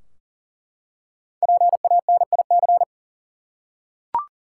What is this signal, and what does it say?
Morse code, heard as a keyed tone, about 40 wpm, 700 Hz. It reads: PANIC
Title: Five Letter Words - 40wpm